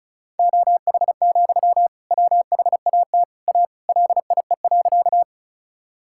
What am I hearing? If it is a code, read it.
Morse code, heard as a keyed tone, about 36 wpm, 700 Hz. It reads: OH, WHAT A LIE.
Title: Sentences from Top 1600 Words 35wpm